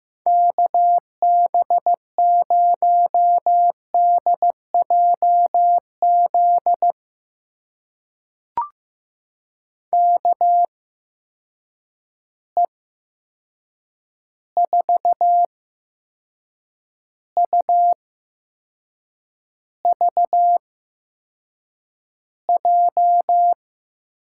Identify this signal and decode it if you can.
Morse code, heard as a keyed tone, about 15 wpm, spaced as for 6 wpm, 700 Hz. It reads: KB0DJZ KE4UVJ
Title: USA Call Signs - 6 Characters - 8x Character Spacing - 15wpm